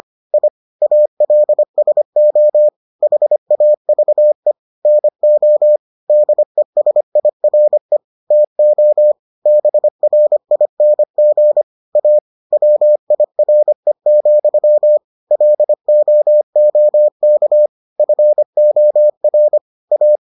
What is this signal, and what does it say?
Morse code, heard as a keyed tone, about 25 wpm, 600 Hz. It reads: I ALSO HAVE NO DESIRE TO BRING A WIRE, LOOK FOR A